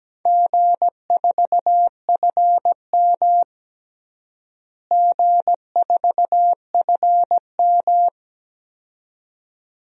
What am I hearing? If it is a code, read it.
Morse code, heard as a keyed tone, about 17 wpm, 700 Hz. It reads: G4FM G4FM
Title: DXCC-100 Call Signs - 4 Characters - Repeated 2x - 17wpm